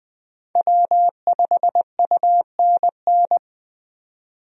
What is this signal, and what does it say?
Morse code, heard as a keyed tone, about 21 wpm, 700 Hz. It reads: W5UNN
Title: USA Call Signs - 5 Characters - 20wpm